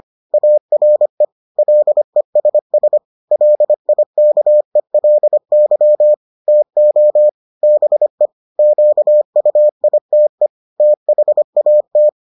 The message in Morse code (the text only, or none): ARE LESS LIKELY TO BE QUITE THAT